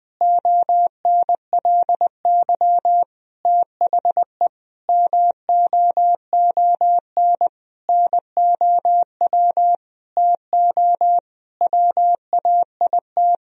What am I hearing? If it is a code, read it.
Morse code, heard as a keyed tone, about 20 wpm, 700 Hz. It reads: ONLY THE MOON NOW TO WAIT
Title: Sentences from Top 1000 Words - Review Part 3 - 20wpm